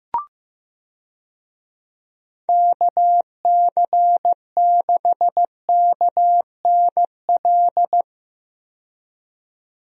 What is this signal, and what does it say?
Morse code, heard as a keyed tone, about 15 wpm, 700 Hz. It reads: KC6KNL
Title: USA Call Signs - 6 Characters - 15wpm